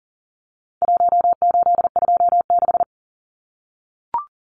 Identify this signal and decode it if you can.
Morse code, heard as a keyed tone, about 40 wpm, 700 Hz. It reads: 1826
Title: Numbers - Contest Style - 4-Digit 40wpm